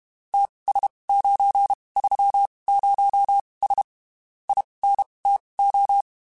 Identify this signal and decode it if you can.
Morse code, heard as a keyed tone, about 31 wpm, spaced as for 23 wpm, 800 Hz. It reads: TS930S INTO
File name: (spaced) / QSOs Spaced-mp3s / kc7sac @30 sp